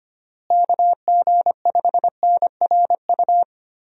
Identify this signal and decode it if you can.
Morse code, heard as a keyed tone, about 25 wpm, 700 Hz. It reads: KG5NRU
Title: USA Call Signs - 6 Characters - Repeated 3x - 25wpm